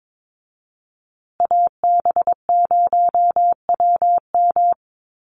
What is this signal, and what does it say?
Morse code, heard as a keyed tone, about 22 wpm, 700 Hz. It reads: AB0WM
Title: USA Call Signs Repeated 2x 22wpm